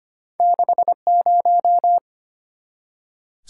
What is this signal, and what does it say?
Morse code, heard as a keyed tone, about 25 wpm, 700 Hz. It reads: 60
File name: Two Letter-Number Combinations - 25wpm